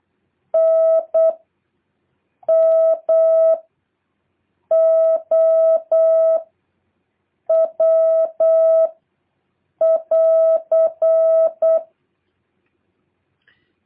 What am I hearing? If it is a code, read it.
Morse code, heard as a keyed tone, about 8 wpm, 650 Hz. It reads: N M O W <AR>
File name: jh8cbh / 2023cw